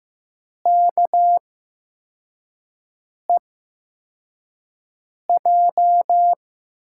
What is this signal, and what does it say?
Morse code, heard as a keyed tone, about 15 wpm, spaced as for 4 wpm, 700 Hz. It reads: KEJ
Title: USA Call Signs - 5 Characters - 8x Character Spacing - 15wpm